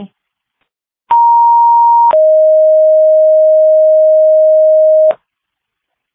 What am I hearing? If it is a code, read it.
Two-tone page, heard as a one-second tone, then a three-second tone, about 940 then 620 Hz. It is heard from 1.1 to 5.1 s.